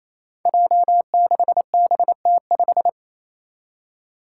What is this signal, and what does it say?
Morse code, heard as a keyed tone, about 28 wpm, 700 Hz. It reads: J6BT5
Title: Five Challenging Letter-Number Combinations 28wpm